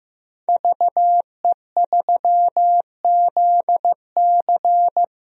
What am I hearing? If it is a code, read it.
Morse code, heard as a keyed tone, about 15 wpm, 700 Hz. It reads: VE3ZC
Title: DXCC-100 Call Signs - 5 Characters - 15wpm